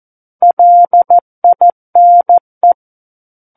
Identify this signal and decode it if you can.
Morse code, heard as a keyed tone, about 14 wpm, 700 Hz. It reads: LINE
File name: Book One / 05-Chapter 5 - 14 WPM